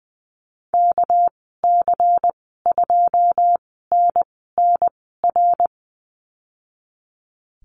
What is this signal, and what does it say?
Morse code, heard as a keyed tone, about 19 wpm, 700 Hz. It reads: K C 2 N N R
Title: 238 Lesson 38g - Letter q - Call Signs 20wpm